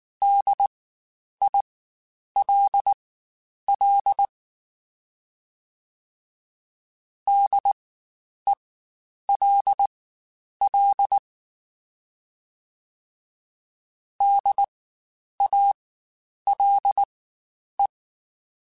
Morse code, heard as a keyed tone, about 19 wpm, spaced as for 9 wpm, 800 Hz. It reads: DILL DELL DALE
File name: Spaced Alphas / Spaced Alphas mp3s / Ds sp @18